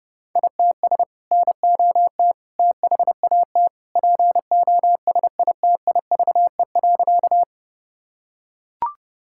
Morse code, heard as a keyed tone, about 30 wpm, 700 Hz. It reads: ITS NOT THAT POSITIVE.
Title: Sentences from Top 1000 Words - Review Part 5 - 30wpm